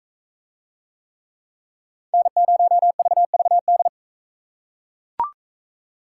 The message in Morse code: N0VVD

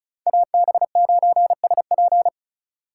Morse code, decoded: AB9SP

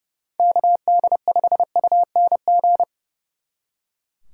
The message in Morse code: KD5UNG

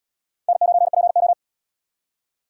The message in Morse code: N9PO

